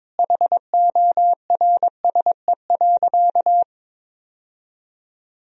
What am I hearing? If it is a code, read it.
Morse code, heard as a keyed tone, about 22 wpm, 700 Hz. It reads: HORSE.
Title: Sentences from Top 1000 Words - Review Part 1 - 22wpm